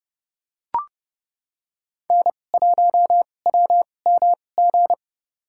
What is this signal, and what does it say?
Morse code, heard as a keyed tone, about 30 wpm, spaced as for 22 wpm, 700 Hz. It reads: N1WMG
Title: USA Call Signs - 5 Characters - 2x Character Spacing - 30wpm